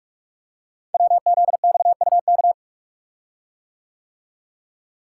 Morse code, heard as a keyed tone, about 44 wpm, 690 Hz. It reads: WZXUK
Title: Code Groups - 5 Letter:Numbers - 45wpm